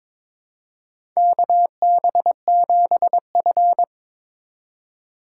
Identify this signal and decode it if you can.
Morse code, heard as a keyed tone, about 22 wpm, 700 Hz. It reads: KB7F
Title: USA Call Signs - 4 Characters - 4x Character Spacing - 22wpm